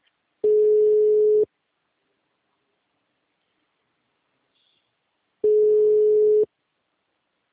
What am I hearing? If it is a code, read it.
Ringback tone (Europe, 425 Hz).